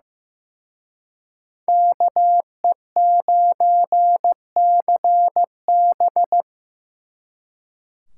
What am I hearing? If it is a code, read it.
Morse code, heard as a keyed tone, about 18 wpm, 700 Hz. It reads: KE9CB
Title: Speed Racing - USA Call Signs 15wpm